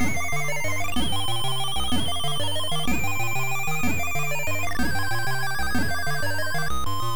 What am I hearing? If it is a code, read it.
There is no code.